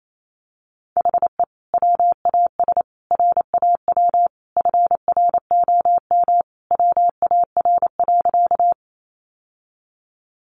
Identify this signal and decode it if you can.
Morse code, heard as a keyed tone, about 28 wpm, 700 Hz. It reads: HE WAS RAW FROM WAR.
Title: Sentences from Top 1500 Words 28wpm